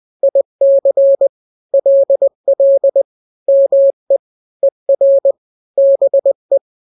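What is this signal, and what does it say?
Morse code, heard as a keyed tone, about 20 wpm, 550 Hz. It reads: IC LL ME ER BE
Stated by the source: Morse drill of two-letter n-grams at 20/18 WPM (Farnsworth), from KK7RR.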